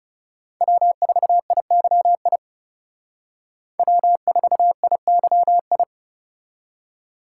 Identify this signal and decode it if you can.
Morse code, heard as a keyed tone, about 36 wpm, 700 Hz. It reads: W4IYI W4IYI
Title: Speed Racing - USA Call Signs 30wpm